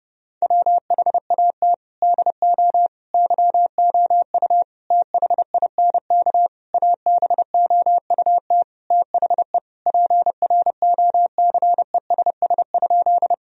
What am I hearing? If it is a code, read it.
Morse code, heard as a keyed tone, about 30 wpm, 700 Hz. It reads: WHAT DO YOU THINK ABOUT THE PROCESS?
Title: Sentences from Top 1000 Words - Review Part 3 - 30wpm